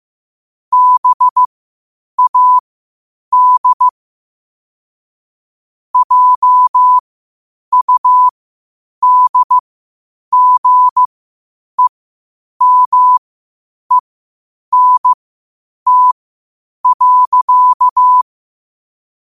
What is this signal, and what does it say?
Morse code, heard as a keyed tone, about 15 wpm, spaced as for 8 wpm, 1000 Hz. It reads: BAD JUDGEMENT.
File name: qotdfiles / 8wpm